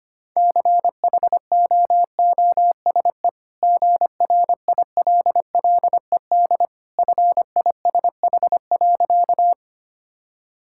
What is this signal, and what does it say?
Morse code, heard as a keyed tone, about 25 wpm, 690 Hz. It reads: CHOOSE GRILLED FISH.
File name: Easy Sentences with Hints - 25wpm